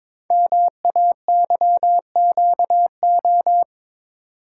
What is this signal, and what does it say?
Morse code, heard as a keyed tone, about 23 wpm, 700 Hz. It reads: MAYQO
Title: Code Groups - 5 Letters - 22wpm